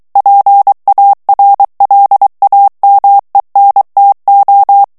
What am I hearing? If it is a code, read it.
Morse code, heard as a keyed tone, about 23 wpm, 800 Hz. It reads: PARLAMENTO
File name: morse mp3 / ESPAGNOL TEXTE EN CLAIR 2 VITESSE 22 MOTS MINUTES